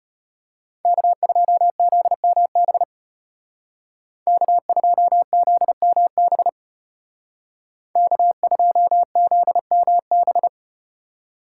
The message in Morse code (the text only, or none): K2ZMB K2ZMB K2ZMB